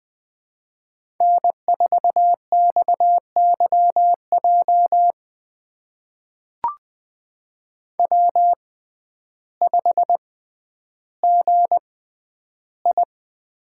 Morse code, heard as a keyed tone, about 21 wpm, spaced as for 11 wpm, 700 Hz. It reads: N4XYJ W5GI
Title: USA Call Signs - 6x Character Spacing - 20wpm